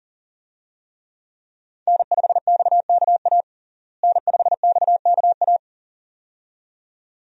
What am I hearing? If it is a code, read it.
Morse code, heard as a keyed tone, about 40 wpm, 700 Hz. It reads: N5XKA N5XKA
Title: USA Call Signs - 5 Characters - Repeated 2x - 40wpm